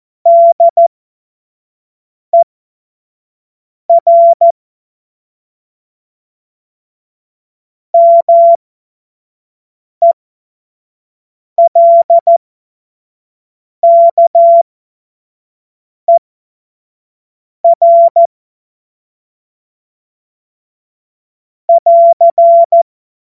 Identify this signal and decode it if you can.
Morse code, heard as a keyed tone, about 14 wpm, spaced as for 5 wpm, 680 Hz. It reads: DER MELKER <AR>
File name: Morsekurs / Witze / Witz 01. März 2023 25 F70